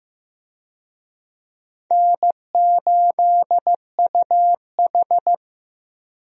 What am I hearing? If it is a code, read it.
Morse code, heard as a keyed tone, about 15 wpm, 700 Hz. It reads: N8UH
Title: USA Call Signs - 4 Characters - Repeated 2x - 15wpm